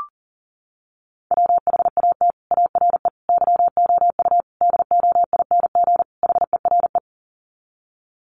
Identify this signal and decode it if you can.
Morse code, heard as a keyed tone, about 40 wpm, 700 Hz. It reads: WHAT ARE YOU DOING HERE
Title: Sets of 5 Words 40wpm